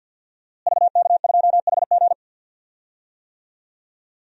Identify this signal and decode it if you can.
Morse code, heard as a keyed tone, about 47 wpm, 700 Hz. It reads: VK2HG
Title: DXCC-100 Call Signs - 5 Characters - Repeated 3x - 50wpm